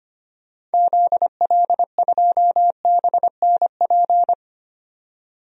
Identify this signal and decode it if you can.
Morse code, heard as a keyed tone, about 25 wpm, 700 Hz. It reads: ZL2BNP
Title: DXCC-100 Call Signs - 6 Characters - 25wpm